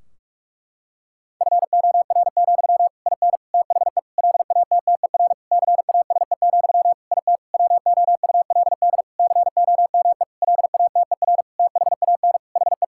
Morse code, heard as a keyed tone, about 46 wpm, 700 Hz. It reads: FOR, IN THE LATTER CASE, IT WOULD COME LATER THAN HE
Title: Sentences from Top 1600 Words 45wpm